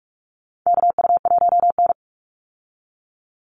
Morse code, heard as a keyed tone, about 44 wpm, 700 Hz. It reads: KU1N
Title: Call Signs Repeated 2x - 45wpm